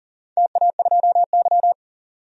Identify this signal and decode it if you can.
Morse code, heard as a keyed tone, about 40 wpm, 700 Hz. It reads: TA2Y
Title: DXCC-100 Call Signs - 4 Characters - 40wpm